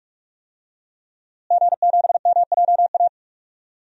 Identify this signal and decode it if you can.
Morse code, heard as a keyed tone, about 43 wpm, 700 Hz. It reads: G7MJA
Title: DXCC-100 Call Signs - 5 Characters - Repeated 2x - 45wpm